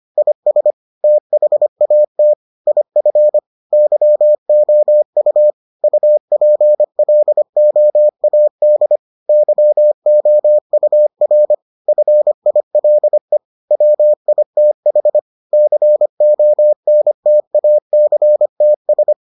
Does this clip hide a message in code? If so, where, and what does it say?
Yes: Morse code from 0.2 to 19.1 s, heard as a keyed tone, about 25 wpm, 600 Hz. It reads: IS THAT IF YOU UPLOAD YOUR FILE WITH CONTACTS